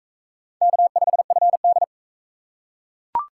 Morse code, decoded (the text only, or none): K5FD